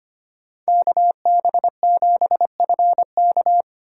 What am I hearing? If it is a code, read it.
Morse code, heard as a keyed tone, about 25 wpm, 700 Hz. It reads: KB7FK
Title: USA Call Signs - 5 Characters - 4x Character Spacing - 25wpm